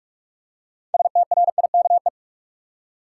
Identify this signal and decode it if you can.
Morse code, heard as a keyed tone, about 44 wpm, 700 Hz. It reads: STRIKE
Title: Six Letter Words - 45wpm